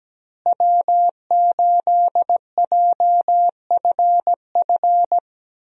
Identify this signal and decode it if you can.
Morse code, heard as a keyed tone, about 17 wpm, 700 Hz. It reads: W8JFF
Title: USA Call Signs - 5 Characters - Repeated 3x - 17wpm